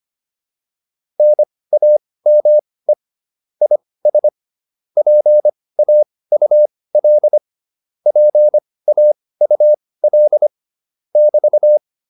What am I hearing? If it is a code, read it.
Morse code, heard as a keyed tone, about 25 wpm, spaced as for 18 wpm, 600 Hz. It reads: NAME IS PAUL PAUL <BT>